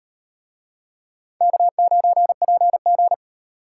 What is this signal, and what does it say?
Morse code, heard as a keyed tone, about 34 wpm, 700 Hz. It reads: K9PG